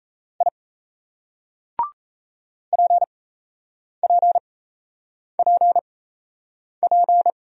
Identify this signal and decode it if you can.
Morse code, heard as a keyed tone, about 37 wpm, spaced as for 8 wpm, 700 Hz. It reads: I PPPP